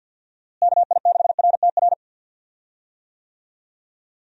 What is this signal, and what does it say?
Morse code, heard as a keyed tone, about 47 wpm, 700 Hz. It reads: KI6RTR
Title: USA Call Signs - 6 Characters - Repeated 2x - 50wpm